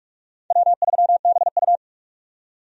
Morse code, heard as a keyed tone, about 46 wpm, 700 Hz. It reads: W3BU
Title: USA Call Signs - 4 Characters - 45wpm